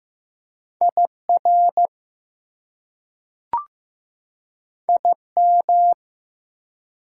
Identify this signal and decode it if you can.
Morse code, heard as a keyed tone, about 15 wpm, 700 Hz. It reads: IR IM
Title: Prefixes and Suffixes 15wpm